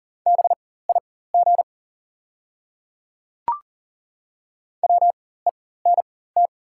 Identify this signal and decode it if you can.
Morse code, heard as a keyed tone, about 40 wpm, spaced as for 19 wpm, 700 Hz. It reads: BIG WENT
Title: 4x Character Spacing - Top 200 Words - 40wpm